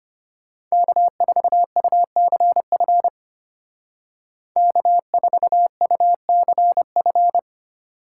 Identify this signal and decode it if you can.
Morse code, heard as a keyed tone, about 33 wpm, 700 Hz. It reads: K4UCF K4UCF